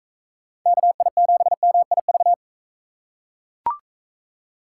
Morse code, heard as a keyed tone, about 43 wpm, 700 Hz. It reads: KI7MIV